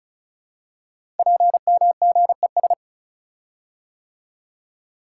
Morse code, heard as a keyed tone, about 36 wpm, 700 Hz. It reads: PMGES
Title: Code Groups - 5 Letters - 35wpm